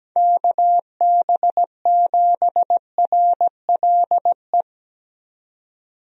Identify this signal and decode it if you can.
Morse code, heard as a keyed tone, about 17 wpm, 700 Hz. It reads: KB7RLE
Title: USA Call Signs - 6 Characters - 17wpm